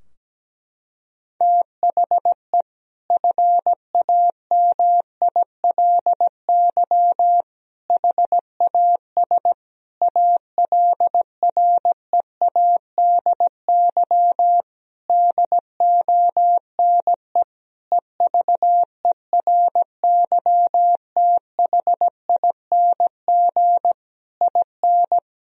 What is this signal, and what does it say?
Morse code, heard as a keyed tone, about 17 wpm, 700 Hz. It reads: THE FAMILY HAS ALREADY DONE EVERYTHING IN
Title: Sentences from Top 1000 Words - Review Part 2 - 17wpm